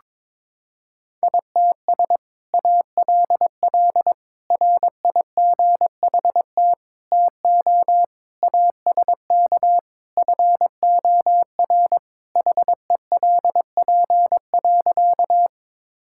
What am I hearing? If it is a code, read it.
Morse code, heard as a keyed tone, about 22 wpm, 700 Hz. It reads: ITS ALL RIGHT TO ASK FOR HELP.
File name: practice / 22wpm / Sentences with 2-Word Phrases - 22wpm